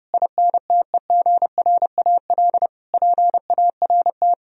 Morse code, heard as a keyed tone, about 40 wpm, 700 Hz. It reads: INTEGRAL PART